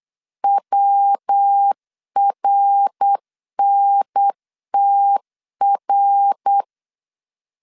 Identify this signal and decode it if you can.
Morse code, heard as a keyed tone, about 8 wpm, 800 Hz. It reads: WRNTR
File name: CW / DET / 1 3 1500C S B 8wpm 4141